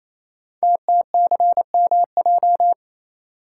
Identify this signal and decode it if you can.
Morse code, heard as a keyed tone, about 28 wpm, 700 Hz. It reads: TTCMJ
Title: Code Groups - 5 Letters - 28wpm